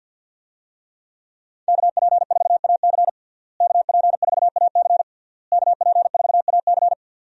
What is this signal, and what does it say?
Morse code, heard as a keyed tone, about 47 wpm, 700 Hz. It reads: KP4AC KP4AC KP4AC
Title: USA Call Signs - 5 Characters - Repeated 3x - 50wpm